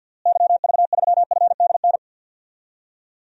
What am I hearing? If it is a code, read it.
Morse code, heard as a keyed tone, about 49 wpm, 700 Hz. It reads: YV3FDN
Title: DXCC-100 Call Signs - 6 Characters - Repeated 3x - 50wpm